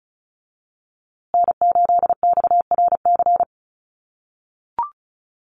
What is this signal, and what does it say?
Morse code, heard as a keyed tone, about 34 wpm, 700 Hz. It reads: N8XRC
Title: USA Call Signs Repeated 3x 35wpm